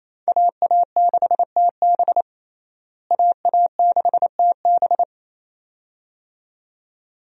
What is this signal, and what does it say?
Morse code, heard as a keyed tone, about 28 wpm, 700 Hz. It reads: AA6TB AA6TB
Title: USA Call Signs - 5 Characters - Repeated 2x - 28wpm